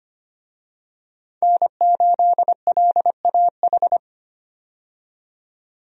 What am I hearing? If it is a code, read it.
Morse code, heard as a keyed tone, about 25 wpm, 700 Hz. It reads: N8LAH
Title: USA Call Signs - 5 Characters - 25wpm